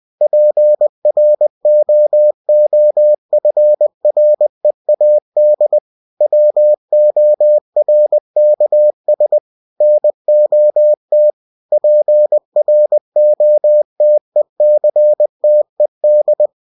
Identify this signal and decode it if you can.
Morse code, heard as a keyed tone, about 20 wpm, 600 Hz. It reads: PROOFREAD WORKS NOT PROTECTED